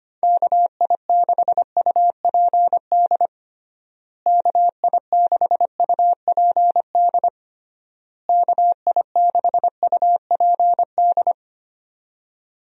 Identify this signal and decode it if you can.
Morse code, heard as a keyed tone, about 25 wpm, 700 Hz. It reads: KI6UPD KI6UPD KI6UPD
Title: USA Call Signs - 6 Characters - Repeated 3x - 25wpm